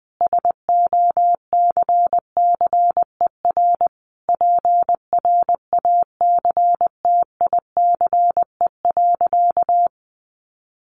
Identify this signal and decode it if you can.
Morse code, heard as a keyed tone, about 20 wpm, 700 Hz. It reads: SOCCER PRACTICE.
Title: Easy Phrases 20wpm